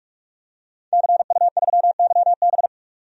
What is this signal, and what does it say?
Morse code, heard as a keyed tone, about 42 wpm, 700 Hz. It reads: CU3YB
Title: DXCC-100 Call Signs - 5 Characters - 45wpm